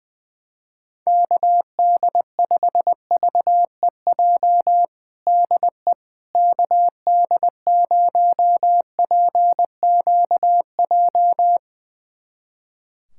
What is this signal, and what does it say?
Morse code, heard as a keyed tone, about 20 wpm, 700 Hz. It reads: KD5VEJ DE KD0PQJ
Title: USA Callsign DE Callsign - 6 Characters - 20wpm